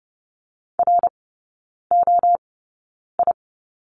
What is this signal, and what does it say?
Morse code, heard as a keyed tone, about 30 wpm, spaced as for 9 wpm, 700 Hz. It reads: ROI
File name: 081 Lesson 14e - Number 2 - Three Characters 30wpm